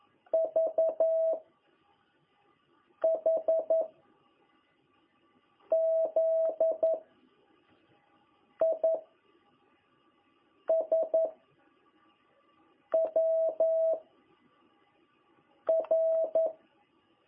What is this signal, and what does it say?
Morse code, heard as a keyed tone, about 11 wpm, 660 Hz. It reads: V H Z I S W R
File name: jh8cbh / 2021cw